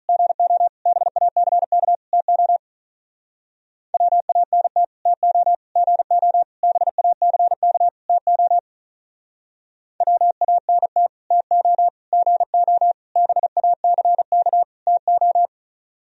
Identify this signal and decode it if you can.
Morse code, heard as a keyed tone, about 46 wpm, 700 Hz. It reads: GO BACK TO WANT TO GO BACK TO WANT TO GO BACK TO